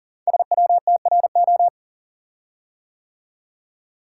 Morse code, heard as a keyed tone, about 39 wpm, 700 Hz. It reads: SWTRO
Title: Code Groups - 5 Letters - 40wpm